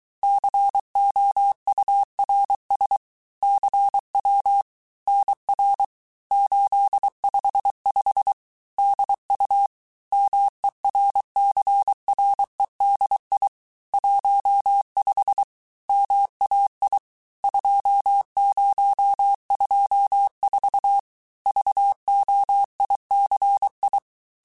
Morse code, heard as a keyed tone, about 23 wpm, 800 Hz. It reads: COURS CW NR 855 DU MERCREDI 15 MAI 2024 VOICI
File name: cours-du-mercredi / audios / cours-855-texte-chiffre-vitesse-22-mpm